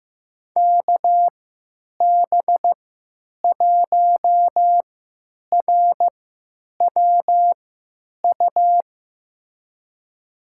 Morse code, heard as a keyed tone, about 16 wpm, spaced as for 9 wpm, 700 Hz. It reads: KB1RWU
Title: USA Call Signs - 6 Characters - 3x Character Spacing - 15wpm